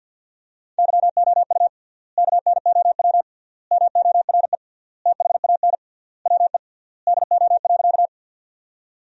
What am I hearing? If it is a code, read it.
Morse code, heard as a keyed tone, about 50 wpm, 700 Hz. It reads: YOU KNOW MORE THAN WE DO.
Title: 3x Word-Spacing - Sentences from Top 100 Words - 50wpm